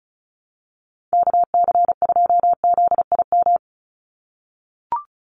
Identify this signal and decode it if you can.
Morse code, heard as a keyed tone, about 34 wpm, 700 Hz. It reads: KC2ZIM
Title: USA Call Signs 35wpm